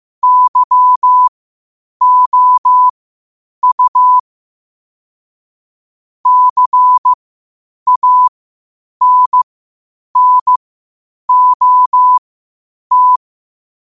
Morse code, heard as a keyed tone, about 15 wpm, spaced as for 8 wpm, 1000 Hz. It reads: YOU CANNOT
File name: qotdfiles / 8wpm